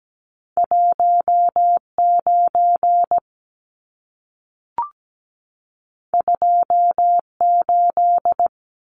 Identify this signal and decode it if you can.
Morse code, heard as a keyed tone, about 17 wpm, 700 Hz. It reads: 19 28
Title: Numbers - 2-Digit 17wpm